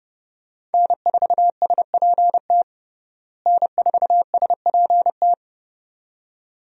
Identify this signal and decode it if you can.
Morse code, heard as a keyed tone, about 30 wpm, 700 Hz. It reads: N4SPT N4SPT
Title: USA Call Signs - 5 Characters - Repeated 2x - 30wpm